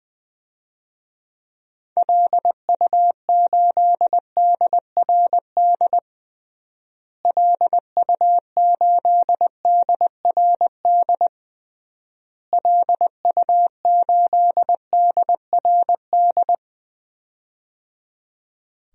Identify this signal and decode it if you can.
Morse code, heard as a keyed tone, about 20 wpm, 700 Hz. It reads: LU8DRD LU8DRD LU8DRD
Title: DXCC-100 Call Signs - 6 Characters - Repeated 3x - 20wpm